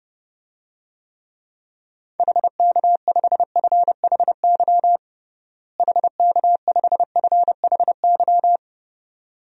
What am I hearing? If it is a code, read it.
Morse code, heard as a keyed tone, about 30 wpm, 700 Hz. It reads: HK5FHY HK5FHY
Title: DXCC-100 Call Signs - 6 Characters - Repeated 2x - 30wpm